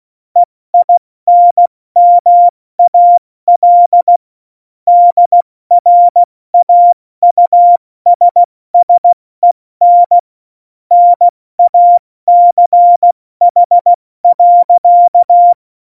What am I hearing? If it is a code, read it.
Morse code, heard as a keyed tone, about 16 wpm, spaced as for 14 wpm, 700 Hz. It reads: EINMAL DRAUSSEN NACH.
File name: Morsekurs / Witze / Witz 29. Januar 2019, 70 F80